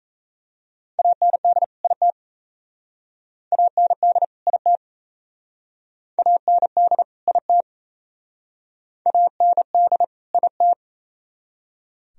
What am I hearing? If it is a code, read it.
Morse code, heard as a keyed tone, about 37 wpm, 700 Hz. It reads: AND IT AND IT AND IT AND IT